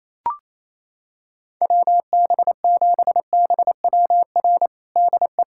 Morse code, heard as a keyed tone, about 28 wpm, 700 Hz. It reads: WB7BWR DE
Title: USA Callsign DE Callsign 28wpm